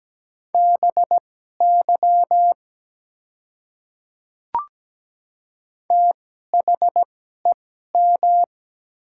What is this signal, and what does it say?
Morse code, heard as a keyed tone, about 17 wpm, spaced as for 12 wpm, 700 Hz. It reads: BY THEM
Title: 2x Character Spacing - Top 100 Words - 17wpm